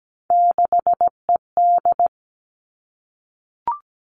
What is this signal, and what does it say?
Morse code, heard as a keyed tone, about 17 wpm, 700 Hz. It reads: 6ED
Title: All Cut Numbers - 3-Digit 17wpm